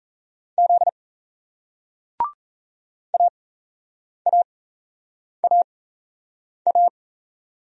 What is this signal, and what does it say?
Morse code, heard as a keyed tone, about 38 wpm, spaced as for 8 wpm, 700 Hz. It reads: Z AAAA